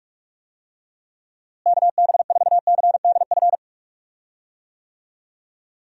Morse code, heard as a keyed tone, about 43 wpm, 700 Hz. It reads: KB4CDF
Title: USA Call Signs - 6 Characters - Repeated 2x - 45wpm